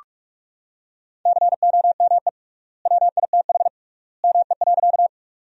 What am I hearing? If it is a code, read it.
Morse code, heard as a keyed tone, about 44 wpm, 700 Hz. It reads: COME WITH ME.
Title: 3x Word-Spacing - Sentences from Top 200 Words - 45wpm